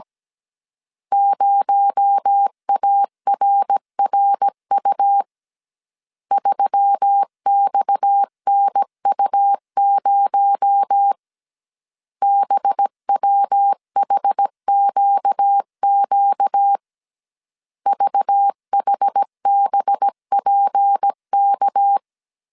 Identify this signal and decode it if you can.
Morse code, heard as a keyed tone, about 17 wpm, 800 Hz. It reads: 0ARRU 3XNU0 BWHQQ VHBPK
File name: 16wpm